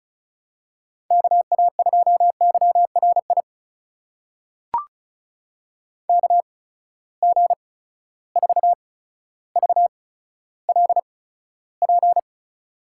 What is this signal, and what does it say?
Morse code, heard as a keyed tone, about 36 wpm, spaced as for 15 wpm, 700 Hz. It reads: KA2YRI KG4VLP
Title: USA Call Signs - 6 Characters - 8x Character Spacing - 35wpm